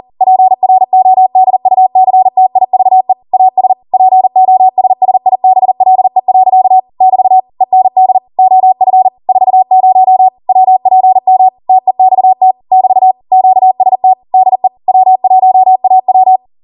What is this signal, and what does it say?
Morse code, heard as a keyed tone, about 40 wpm, 750 Hz. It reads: PRODUCTIVE AS POSSIBLE. <BT> END OF 40 WPM TEXT <BT> QST DE W1AW